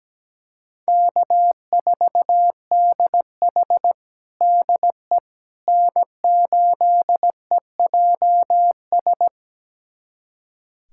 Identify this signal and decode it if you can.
Morse code, heard as a keyed tone, about 17 wpm, 700 Hz. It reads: K4DH DE N8EJS
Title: USA Callsign DE Callsign 17wpm